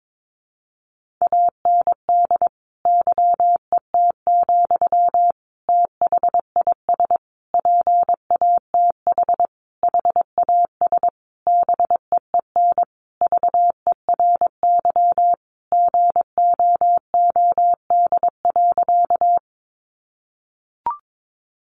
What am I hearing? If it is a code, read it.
Morse code, heard as a keyed tone, about 22 wpm, 700 Hz. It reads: AND YET, THIS PATH HAS BEEN VERY GOOD.
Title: Sentences from Top 1400 Words 22wpm